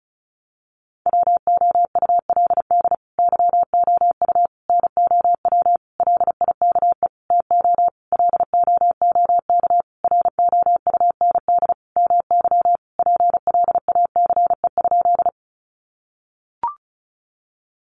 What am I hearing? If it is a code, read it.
Morse code, heard as a keyed tone, about 34 wpm, 700 Hz. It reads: WOULD YOU NOW LIKE TO LOOK ROUND MY PLACE?
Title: Sentences from Top 200 Words 35wpm